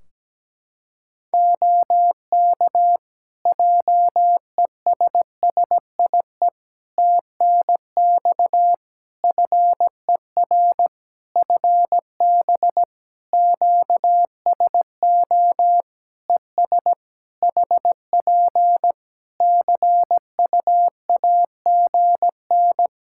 Morse code, heard as a keyed tone, about 17 wpm, 700 Hz. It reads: OK JESSIE TNX FER FB QSO ES HP CUAGN
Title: Practice QSO - Exchange 4 - Ending - 17wpm